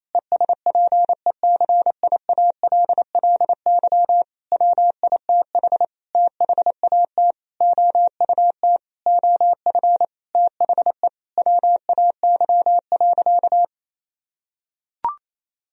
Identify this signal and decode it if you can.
Morse code, heard as a keyed tone, about 28 wpm, 700 Hz. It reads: ESPECIALLY WITH THAT OUT OF THE WAY.
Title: Sentences from Top 1000 Words - Review Part 4 - 28wpm